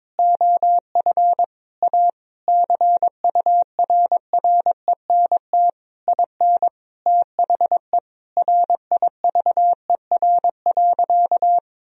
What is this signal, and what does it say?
Morse code, heard as a keyed tone, about 22 wpm, 700 Hz. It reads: OF A CURRENT IN THE RIVER.
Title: Sentences from Top 1000 Words - Review Part 4 - 22wpm